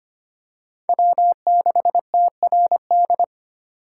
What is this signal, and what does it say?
Morse code, heard as a keyed tone, about 25 wpm, 700 Hz. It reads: W6TRD
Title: USA Call Signs - 5 Characters - 4x Character Spacing - 25wpm